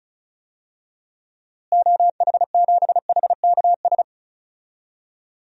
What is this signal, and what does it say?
Morse code, heard as a keyed tone, about 34 wpm, 700 Hz. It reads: OH7HKS
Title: DXCC-100 Call Signs - 6 Characters - Repeated 3x - 35wpm